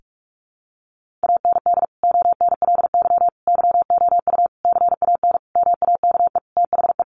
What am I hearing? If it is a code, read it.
Morse code, heard as a keyed tone, about 44 wpm, 700 Hz. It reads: AND ONLY YOU CAN MAKE THE
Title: Sentences from Top 1400 Words 45wpm